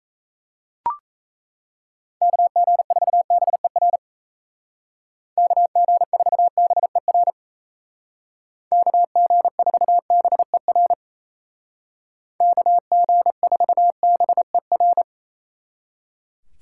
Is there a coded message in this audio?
Yes: Morse code from 2.2 to 15.0 s, heard as a keyed tone, about 36 wpm, 700 Hz. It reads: KG4BER KG4BER KG4BER KG4BER